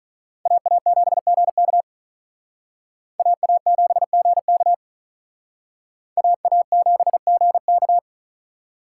Morse code, heard as a keyed tone, about 43 wpm, 700 Hz. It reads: AA7GK AA7GK AA7GK
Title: Speed Racing - USA 2x2 Call Signs 35wpm